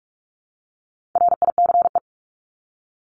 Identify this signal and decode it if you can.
Morse code, heard as a keyed tone, about 44 wpm, 700 Hz. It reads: RICE
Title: Top 1501-1750 Words 45wpm